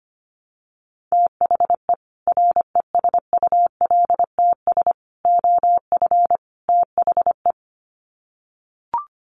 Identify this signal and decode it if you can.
Morse code, heard as a keyed tone, about 25 wpm, 700 Hz. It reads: THE RESULTS OF THE
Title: Sets of 4 Words 25wpm